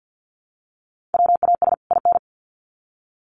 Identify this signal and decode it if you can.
Morse code, heard as a keyed tone, about 50 wpm, 700 Hz. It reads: WAS IN
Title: Sets of 2 Words 50wpm